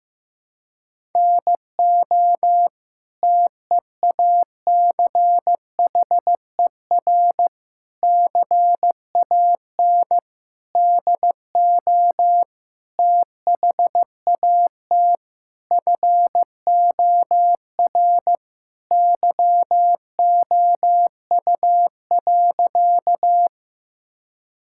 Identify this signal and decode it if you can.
Morse code, heard as a keyed tone, about 15 wpm, 700 Hz. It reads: NO TEACHER CAN DO THAT FOR YOU.
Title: Sentences from Top 1000 Words - Review Part 6 - 15wpm